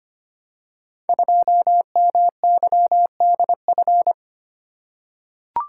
Morse code, heard as a keyed tone, about 25 wpm, 700 Hz. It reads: 2MYDF
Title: Code Groups - 5 Letter:Numbers - 25wpm